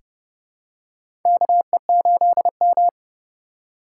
Morse code, heard as a keyed tone, about 30 wpm, 700 Hz. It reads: KE8M
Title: K1USN SST QSO Practice - 30wpm